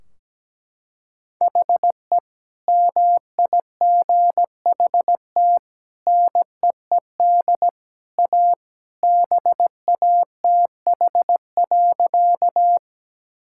Morse code, heard as a keyed tone, about 17 wpm, 700 Hz. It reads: HE MIGHT NEED A BATH.
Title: Sentences from Top 1600 Words 17wpm